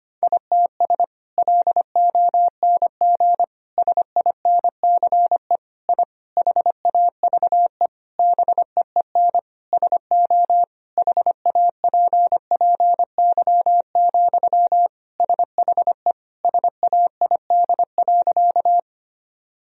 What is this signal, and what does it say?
Morse code, heard as a keyed tone, about 25 wpm, 700 Hz. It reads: ITS LONG SINCE I HAVE BEEN SO HAPPY, SHE SAID.
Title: Sentences from Top 1000 Words - Review Part 1 - 25wpm